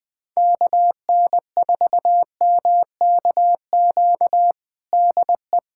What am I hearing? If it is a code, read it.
Morse code, heard as a keyed tone, about 21 wpm, 700 Hz. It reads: KN4MKQ DE
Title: USA Callsign DE Callsign - 6 Characters - 20wpm